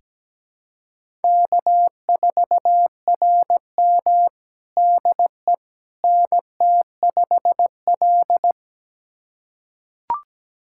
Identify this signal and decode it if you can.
Morse code, heard as a keyed tone, about 17 wpm, 700 Hz. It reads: K4RM DE NT5L
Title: USA Callsign DE Callsign - 4 Characters - 17wpm